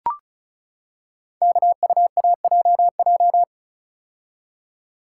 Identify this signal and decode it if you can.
Morse code, heard as a keyed tone, about 36 wpm, 700 Hz. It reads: KUAJJ
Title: Code Groups - 5 Letters - 35wpm